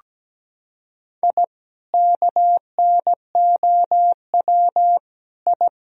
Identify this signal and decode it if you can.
Morse code, heard as a keyed tone, about 17 wpm, 700 Hz. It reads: I KNOW I
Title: Sentences from Top 1000 Words - Review Part 3 - 17wpm